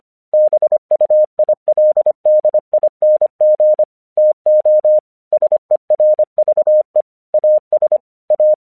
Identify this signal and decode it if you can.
Morse code, heard as a keyed tone, about 25 wpm, 610 Hz. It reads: BUILDING TO SERVE AS A
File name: 1 English 25wpm / AQC07